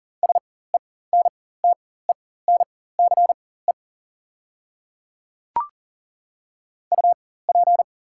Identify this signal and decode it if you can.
Morse code, heard as a keyed tone, about 39 wpm, spaced as for 19 wpm, 690 Hz. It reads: SENTENCE UP
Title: 4x Character Spacing - Top 200 Words - 40wpm